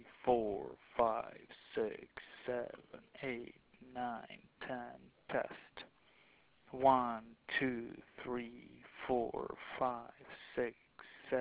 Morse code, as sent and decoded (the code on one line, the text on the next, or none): none